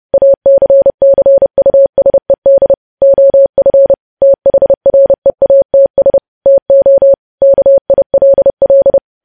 Morse code, heard as a keyed tone, about 30 wpm, 550 Hz. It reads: ACCUSED OF THREATS TO KILL
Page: news headlines in morse code at 30 wpm